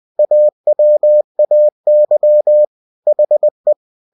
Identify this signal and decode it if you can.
Morse code, heard as a keyed tone, about 20 wpm, 600 Hz. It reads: AWAY HE